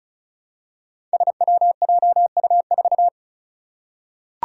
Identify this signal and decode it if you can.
Morse code, heard as a keyed tone, about 36 wpm, 700 Hz. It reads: SWJU4
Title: Five Challenging Letter-Number Combinations 35wpm